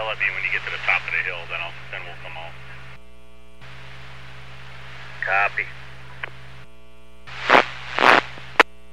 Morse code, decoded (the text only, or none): none